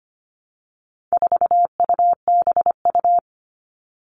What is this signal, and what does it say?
Morse code, heard as a keyed tone, about 25 wpm, 700 Hz. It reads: 4UBU
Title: All Cut Numbers - 4-Digit 25wpm